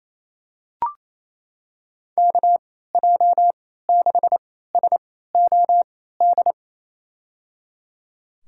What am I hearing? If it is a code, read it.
Morse code, heard as a keyed tone, about 28 wpm, spaced as for 16 wpm, 700 Hz. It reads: KJ6SOD
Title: USA Call Signs - 6 Characters - 3x Character Spacing - 28wpm